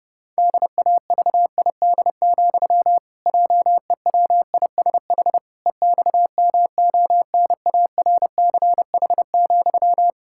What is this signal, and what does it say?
Morse code, heard as a keyed tone, about 32 wpm, 700 Hz. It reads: DAVID, JEWISH EXMONARCH,
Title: Pangram Challenge v2 - 30wpm